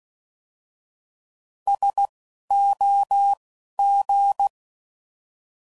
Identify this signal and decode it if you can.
Morse code, heard as a keyed tone, about 16 wpm, spaced as for 12 wpm, 800 Hz. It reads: SOG